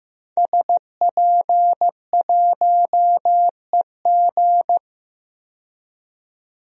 Morse code, heard as a keyed tone, about 15 wpm, 700 Hz. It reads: SP1EG
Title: DXCC-100 Call Signs - 5 Characters - 15wpm